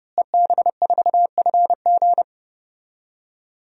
Morse code, heard as a keyed tone, about 30 wpm, 700 Hz. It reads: EB4FG
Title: DXCC-100 Call Signs - 5 Characters - Repeated 3x - 30wpm